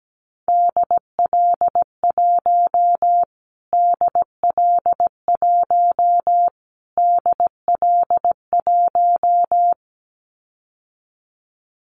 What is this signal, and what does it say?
Morse code, heard as a keyed tone, about 17 wpm, 700 Hz. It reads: DL1 DL1 DL1
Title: Callsign Prefixes for Top 25 DXCC - Repeated 3x - 17wpm